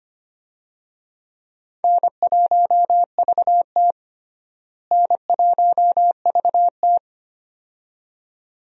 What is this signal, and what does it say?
Morse code, heard as a keyed tone, about 25 wpm, 700 Hz. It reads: N1VT N1VT
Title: USA Call Signs - 4 Characters - Repeated 2x - 25wpm